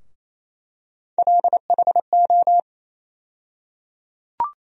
Morse code, heard as a keyed tone, about 27 wpm, 700 Hz. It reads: LHO
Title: USA Call Sign Postfixes - 3 Letters 28wpm